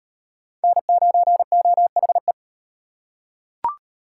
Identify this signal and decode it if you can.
Morse code, heard as a keyed tone, about 36 wpm, 700 Hz. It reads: N9OHE